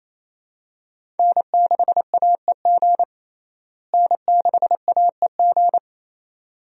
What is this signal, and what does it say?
Morse code, heard as a keyed tone, about 28 wpm, 700 Hz. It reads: N6AEG N6AEG
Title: USA Call Signs - 5 Characters - Repeated 2x - 28wpm